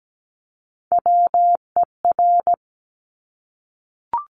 Morse code, heard as a keyed tone, about 17 wpm, 700 Hz. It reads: WER
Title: Top Three Letter Combinations in Words 17wpm